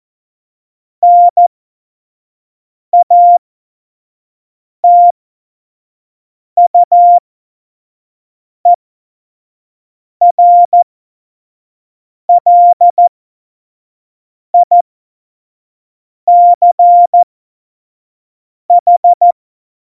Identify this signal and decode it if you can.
Morse code, heard as a keyed tone, about 14 wpm, spaced as for 5 wpm, 700 Hz. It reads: NATUERLICH